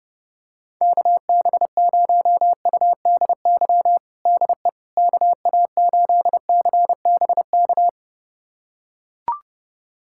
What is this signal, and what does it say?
Morse code, heard as a keyed tone, about 32 wpm, 700 Hz. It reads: KB0UDY DE KA8CBK
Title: USA Callsign DE Callsign - 6 Characters - 30wpm